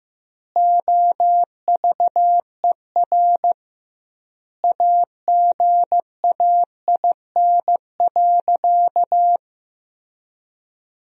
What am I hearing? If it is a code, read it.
Morse code, heard as a keyed tone, about 15 wpm, 700 Hz. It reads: OVER AGAIN.
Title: 2x Word-Spacing - Sentences from Top 500 Words - 15wpm